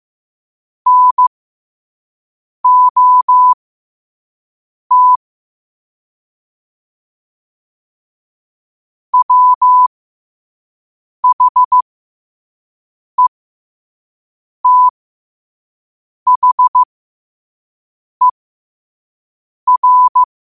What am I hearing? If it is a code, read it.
Morse code, heard as a keyed tone, about 15 wpm, spaced as for 5 wpm, 1000 Hz. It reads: NOT WHETHER
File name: qotdfiles / 5wpm